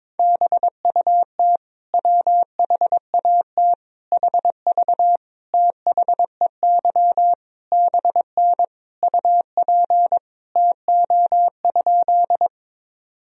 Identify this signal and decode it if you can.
Morse code, heard as a keyed tone, about 22 wpm, 700 Hz. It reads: BUT WHAT HV THEY BN UP TO?
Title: Sentences from Top 100 Words and Abbreviations 22wpm